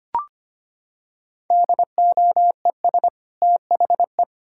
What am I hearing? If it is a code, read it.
Morse code, heard as a keyed tone, about 25 wpm, 700 Hz. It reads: DOES THE
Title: Sentences from Top 1000 Words - Review Part 5 - 25wpm